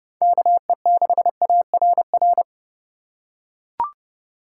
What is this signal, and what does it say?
Morse code, heard as a keyed tone, about 31 wpm, 700 Hz. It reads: KE6ARR
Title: USA Call Signs - 6 Characters - 3x Character Spacing - 30wpm